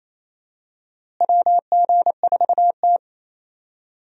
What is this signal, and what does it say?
Morse code, heard as a keyed tone, about 28 wpm, 700 Hz. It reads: WG4T
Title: USA Call Signs - 4 Characters - Repeated 2x - 28wpm